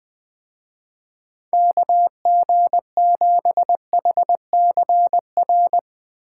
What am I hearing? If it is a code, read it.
Morse code, heard as a keyed tone, about 20 wpm, 700 Hz. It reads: KG7HCR
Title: Speed Racing - USA 2x3 Call Signs 20wpm